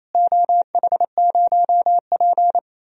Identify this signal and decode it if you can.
Morse code, heard as a keyed tone, about 28 wpm, 700 Hz. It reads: OH0P
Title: DXCC-100 Call Signs - 4 Characters - Repeated 2x - 28wpm